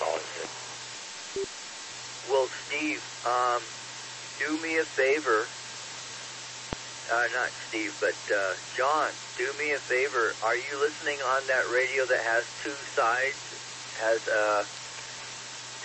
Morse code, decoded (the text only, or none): none